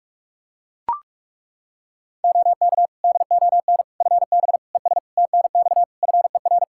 Morse code, heard as a keyed tone, about 41 wpm, 700 Hz. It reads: OK DON FB ES TNX FER
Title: Practice QSO - Exchange 2 - Info - 45wpm